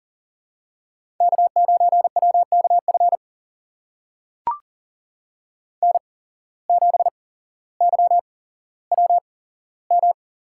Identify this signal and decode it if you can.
Morse code, heard as a keyed tone, about 38 wpm, spaced as for 17 wpm, 700 Hz. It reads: K9WKF N7YWM
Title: USA Call Signs - 5 Characters - 8x Character Spacing - 40wpm